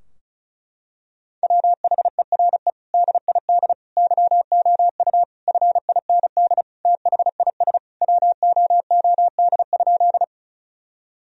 Morse code, heard as a keyed tone, about 36 wpm, 700 Hz. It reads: WHERE DID YOU FIND THIS WOOD?
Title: Sentences from Top 1000 Words - Review Part 2 - 35wpm